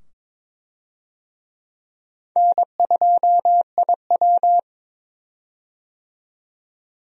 Morse code, heard as a keyed tone, about 22 wpm, 700 Hz. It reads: N2IW
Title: USA Call Signs - 4 Characters - 22wpm